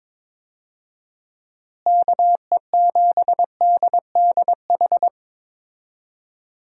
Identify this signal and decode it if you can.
Morse code, heard as a keyed tone, about 22 wpm, 700 Hz. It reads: KE7DDH
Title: USA Call Signs - 6 Characters - Repeated 3x - 22wpm